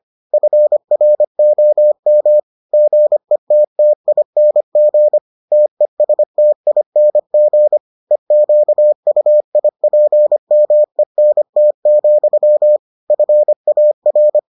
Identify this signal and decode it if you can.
Morse code, heard as a keyed tone, about 25 wpm, 600 Hz. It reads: FROM GETTING TESTING EQUIPMENT, FAR